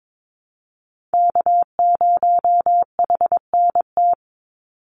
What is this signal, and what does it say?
Morse code, heard as a keyed tone, about 22 wpm, 700 Hz. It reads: K0HNT
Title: USA Call Signs 22wpm